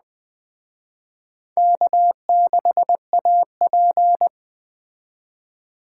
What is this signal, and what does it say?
Morse code, heard as a keyed tone, about 20 wpm, 700 Hz. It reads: K6AP
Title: Speed Racing - USA 1x2 Call Signs 17wpm